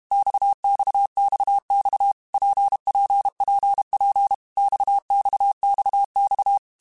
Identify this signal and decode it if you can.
Morse code, heard as a keyed tone, about 32 wpm, 800 Hz. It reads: XXXX PPPP XXXX
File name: Basic Letter Groups By7Groups / RKPX / RKPX@30